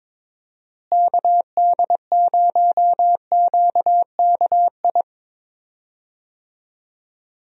Morse code, heard as a keyed tone, about 22 wpm, 700 Hz. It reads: KD0QKI
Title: USA Call Signs - 6 Characters - Repeated 2x - 22wpm